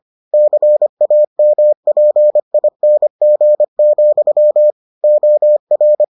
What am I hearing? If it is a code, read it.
Morse code, heard as a keyed tone, about 25 wpm, 600 Hz. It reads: CAMPING, OR